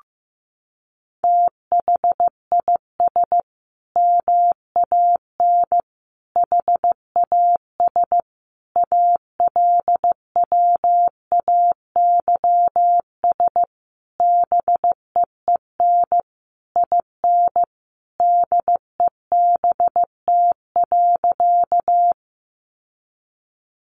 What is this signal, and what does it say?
Morse code, heard as a keyed tone, about 15 wpm, 700 Hz. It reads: THIS MAN HAS ALWAYS BEEN IN DEBT.
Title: Sentences from Top 1300 Words 15wpm